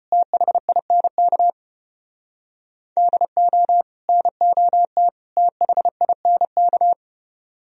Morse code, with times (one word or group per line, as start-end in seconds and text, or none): 0.1-1.5 THINK
3.0-3.8 DO
4.1-5.1 NOT
5.4-6.9 THINK